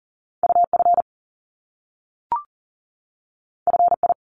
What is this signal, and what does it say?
Morse code, heard as a keyed tone, about 38 wpm, 700 Hz. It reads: UF FI
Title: Top Two Letter Combinations in Words 40wpm